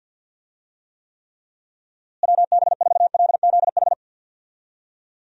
Morse code, heard as a keyed tone, about 51 wpm, 700 Hz. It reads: WB4LZH
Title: USA Call Signs - 6 Characters - Repeated 3x - 50wpm